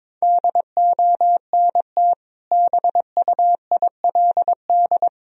Morse code, heard as a keyed tone, about 22 wpm, 700 Hz. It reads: DONT BUILD
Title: Sentences from Top 1000 Words - Review Part 1 - 22wpm